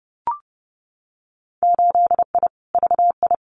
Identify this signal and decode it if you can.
Morse code, heard as a keyed tone, about 31 wpm, 700 Hz. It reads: 8I VI
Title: Winter Field Day Exchanges Encore 30wpm